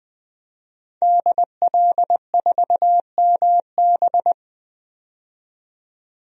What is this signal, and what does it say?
Morse code, heard as a keyed tone, about 20 wpm, 700 Hz. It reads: DL4MB
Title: DXCC-100 Call Signs - 5 Characters - 20wpm